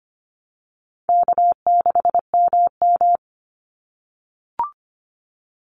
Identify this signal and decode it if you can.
Morse code, heard as a keyed tone, about 25 wpm, 700 Hz. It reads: K6MM
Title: Easy Call Signs - 25wpm